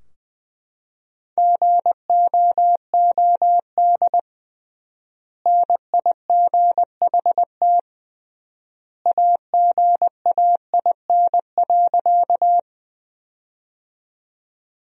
Morse code, heard as a keyed tone, about 20 wpm, 700 Hz. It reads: GOOD NIGHT AGAIN.
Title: 3x Word-Spacing - Sentences from Top 300 Words - 20wpm